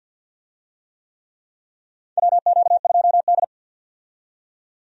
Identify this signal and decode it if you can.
Morse code, heard as a keyed tone, about 52 wpm, 700 Hz. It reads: WQ2D
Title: USA Call Signs - 4 Characters - 50wpm